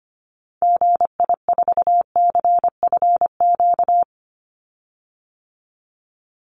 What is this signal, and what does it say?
Morse code, heard as a keyed tone, about 25 wpm, 700 Hz. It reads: GI4CFQ
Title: Call Signs Repeated 3x - 25wpm